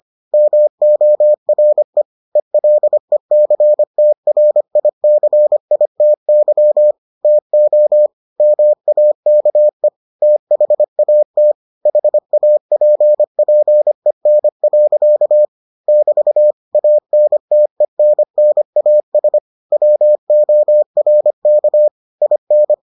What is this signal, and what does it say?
Morse code, heard as a keyed tone, about 25 wpm, 600 Hz. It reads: MORE ELECTRICITY TO MAKE THAT HAPPEN. <BT> ANTENNAS WORK IN